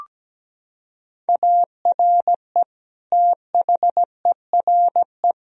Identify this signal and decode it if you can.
Morse code, heard as a keyed tone, about 17 wpm, 700 Hz. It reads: ARE THERE
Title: Sentences from Top 1000 Words - Review Part 1 - 17wpm